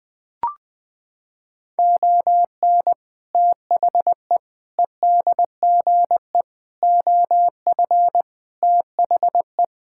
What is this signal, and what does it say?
Morse code, heard as a keyed tone, about 19 wpm, 700 Hz. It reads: ON THE EDGE OF THE
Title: Sentences with 3-Word Phrases - 20wpm